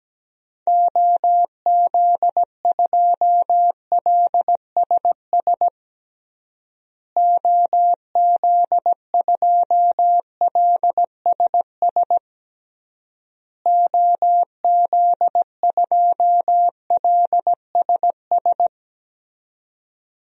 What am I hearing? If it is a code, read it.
Morse code, heard as a keyed tone, about 17 wpm, 700 Hz. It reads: OZ2LSS OZ2LSS OZ2LSS
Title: DXCC-100 Call Signs - 6 Characters - Repeated 3x - 17wpm